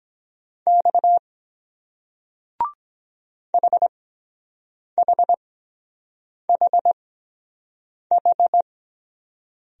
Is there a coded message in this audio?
Yes: Morse code from 0.7 to 8.6 s, heard as a keyed tone, about 23 wpm, spaced as for 7 wpm, 700 Hz. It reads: X HHHH